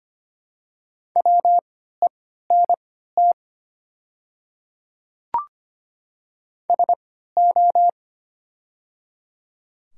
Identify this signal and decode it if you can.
Morse code, heard as a keyed tone, about 25 wpm, spaced as for 14 wpm, 700 Hz. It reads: WENT SO
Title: 3x Character Spacing - Top 300 Words - 25wpm